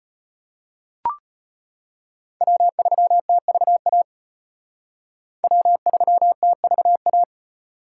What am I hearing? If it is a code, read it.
Morse code, heard as a keyed tone, about 32 wpm, 700 Hz. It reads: W3TVA W3TVA